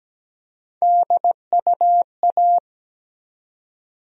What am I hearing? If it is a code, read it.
Morse code, heard as a keyed tone, about 17 wpm, 700 Hz. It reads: DUA
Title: USA Call Sign Postfixes - 3 Letters 17wpm